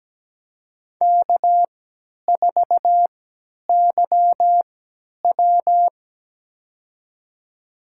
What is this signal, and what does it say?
Morse code, heard as a keyed tone, about 17 wpm, spaced as for 10 wpm, 700 Hz. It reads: K4YW
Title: USA Call Signs - 4 Characters - 3x Character Spacing - 17wpm